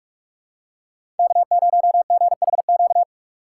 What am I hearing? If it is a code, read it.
Morse code, heard as a keyed tone, about 46 wpm, 700 Hz. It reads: K0GHQ
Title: USA Call Signs - 5 Characters - 3x Character Spacing - 45wpm